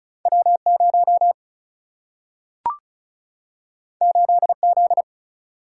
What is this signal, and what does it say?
Morse code, heard as a keyed tone, about 36 wpm, 700 Hz. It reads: W 0 8 Z
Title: Single-Letter Single-Number 35wpm